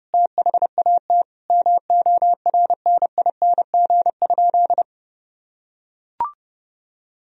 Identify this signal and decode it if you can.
Morse code, heard as a keyed tone, about 30 wpm, 700 Hz. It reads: THAT MORNING?
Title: Sentences from Top 1000 Words - Review Part 3 - 30wpm